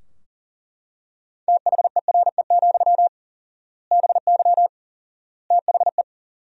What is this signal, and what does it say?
Morse code, heard as a keyed tone, about 37 wpm, 700 Hz. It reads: THERE, BY THE
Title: 4x Word-Spacing - Sentences from Top 500 Words - 40wpm